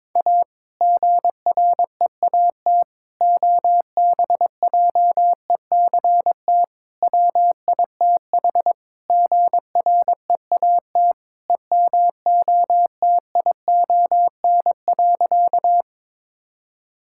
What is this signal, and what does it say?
Morse code, heard as a keyed tone, about 22 wpm, 700 Hz. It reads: A GREAT OBJECT WITH GREAT EMOTION.
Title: Sentences from Top 1600 Words 22wpm